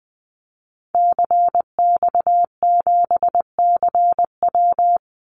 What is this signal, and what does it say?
Morse code, heard as a keyed tone, about 20 wpm, 700 Hz. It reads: CX7CW
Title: Call Signs Repeated 3x - 20wpm